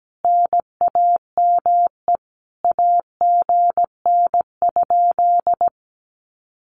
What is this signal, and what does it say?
Morse code, heard as a keyed tone, about 17 wpm, 700 Hz. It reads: NAME AGN?
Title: Common QSO Phrases Encore 17wpm